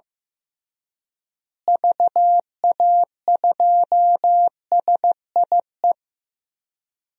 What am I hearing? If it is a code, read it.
Morse code, heard as a keyed tone, about 14 wpm, 700 Hz. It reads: VA2SIE
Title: DXCC-100 Call Signs - 6 Characters - Repeated 3x - 15wpm